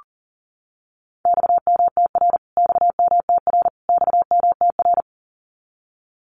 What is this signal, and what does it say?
Morse code, heard as a keyed tone, about 40 wpm, 700 Hz. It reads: XMTR XMTR XMTR
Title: Abbreviations Repeated 3x - 40wpm